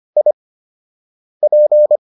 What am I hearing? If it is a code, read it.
Morse code, heard as a keyed tone, about 25 wpm, spaced as for 7 wpm, 600 Hz. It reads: IP